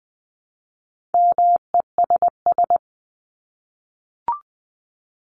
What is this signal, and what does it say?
Morse code, heard as a keyed tone, about 20 wpm, 700 Hz. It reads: MESS
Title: Top 1751-2000 Words 20wpm